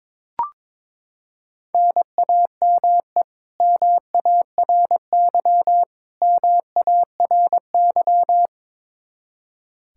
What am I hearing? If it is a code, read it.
Morse code, heard as a keyed tone, about 22 wpm, 700 Hz. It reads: NAME MARY MARY
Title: K1USN SST QSO Practice 22wpm